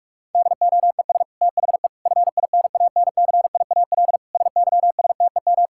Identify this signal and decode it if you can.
Morse code, heard as a keyed tone, about 46 wpm, 690 Hz. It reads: DOES THE FINANCIAL SYSTEM
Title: Sentences from Top 1000 Words - Review Part 5 - 45wpm